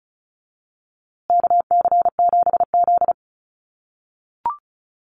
Morse code, heard as a keyed tone, about 34 wpm, 700 Hz. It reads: KC7Z
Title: USA Call Signs Repeated 3x 35wpm